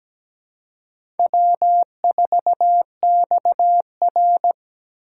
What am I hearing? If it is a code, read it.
Morse code, heard as a keyed tone, about 17 wpm, 700 Hz. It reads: W4XR
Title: USA Call Signs - 2x Character Spacing - 17wpm